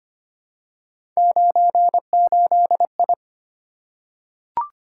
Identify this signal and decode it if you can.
Morse code, heard as a keyed tone, about 25 wpm, 700 Hz. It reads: 98I
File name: Three Letter-Number Combinations - 25wpm